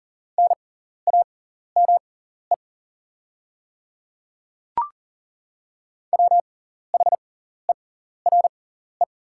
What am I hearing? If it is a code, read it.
Morse code, heard as a keyed tone, about 40 wpm, spaced as for 14 wpm, 700 Hz. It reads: NAME WHERE
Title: 6x Character Spacing - Top 200 Words - 40wpm